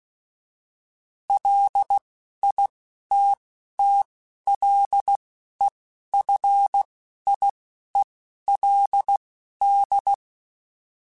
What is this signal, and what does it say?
Morse code, heard as a keyed tone, about 16 wpm, spaced as for 11 wpm, 800 Hz. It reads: LITTLEFIELD